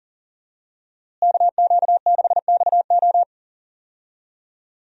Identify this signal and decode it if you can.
Morse code, heard as a keyed tone, about 40 wpm, 700 Hz. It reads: KQ6XO
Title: USA Call Signs - 5 Characters - Repeated 2x - 40wpm